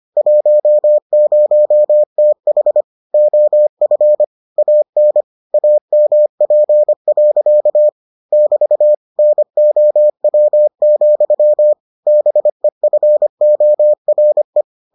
Morse code, heard as a keyed tone, about 25 wpm, 600 Hz. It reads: 10TH OF AN AMP. <BT> NOW, BEFORE